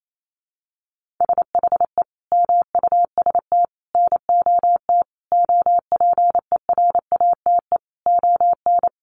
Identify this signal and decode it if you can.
Morse code, heard as a keyed tone, about 28 wpm, 700 Hz. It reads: SHE MUST NOT OPERATE ON
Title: Sentences from Top 1500 Words 28wpm